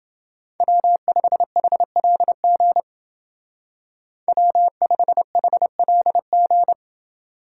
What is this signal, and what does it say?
Morse code, heard as a keyed tone, about 29 wpm, 700 Hz. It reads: W5HLG W5HLG